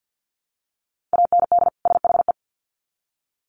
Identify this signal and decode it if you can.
Morse code, heard as a keyed tone, about 50 wpm, 700 Hz. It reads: AND SHE
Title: Sets of 2 Words 50wpm